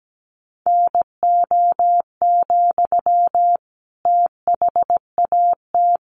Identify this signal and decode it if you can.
Morse code, heard as a keyed tone, about 17 wpm, 700 Hz. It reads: NO, THAT
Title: Sentences from Top 600 Words 17wpm